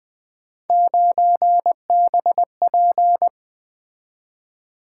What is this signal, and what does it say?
Morse code, heard as a keyed tone, about 24 wpm, 700 Hz. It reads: 9BP
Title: Speed-Racing - Three Letter-Number Combinations - 20wpm